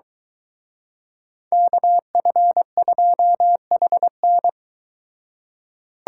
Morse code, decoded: KF2HN

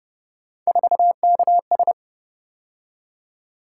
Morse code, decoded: 4KS

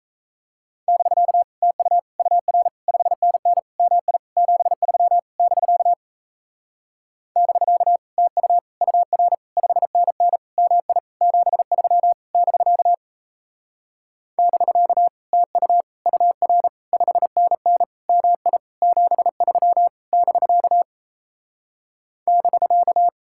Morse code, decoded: <BK> TU UR 5NN MI 73 <BK> <BK> TU UR 5NN MI 73 <BK> <BK> TU UR 5NN MI 73 <BK> <BK>

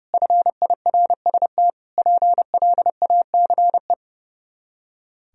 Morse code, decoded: FIRST PLACE

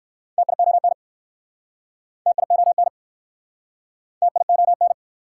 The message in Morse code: TION TION TION

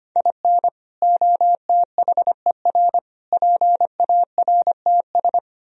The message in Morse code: IN OTHER PARTS

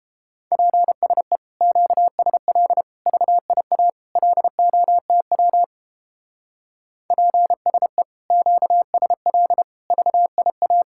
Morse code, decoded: PSE QSL VIA LOTW PSE QSL VIA

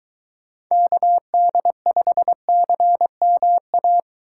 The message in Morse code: KD5CMA

